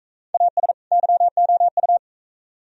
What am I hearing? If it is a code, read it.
Morse code, heard as a keyed tone, about 42 wpm, 700 Hz. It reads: AS YOU